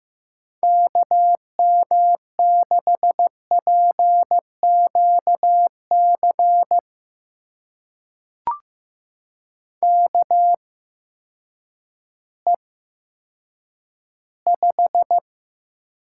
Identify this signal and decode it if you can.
Morse code, heard as a keyed tone, about 15 wpm, spaced as for 9 wpm, 700 Hz. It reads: KM6PQC KE5
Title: USA Call Signs - 6 Characters - 8x Character Spacing - 15wpm